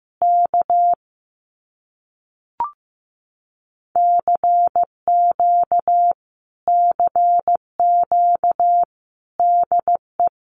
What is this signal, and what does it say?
Morse code, heard as a keyed tone, about 15 wpm, 700 Hz. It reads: K CQ CQ DE
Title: Common QSO Phrases Encore 15wpm